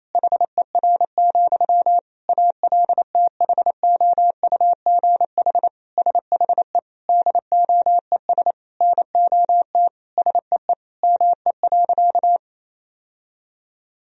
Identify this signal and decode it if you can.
Morse code, heard as a keyed tone, about 28 wpm, 700 Hz. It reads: HER, ALTHOUGH SHE DOES NOT SEE ME.
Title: Sentences from Top 1000 Words - Review Part 3 - 28wpm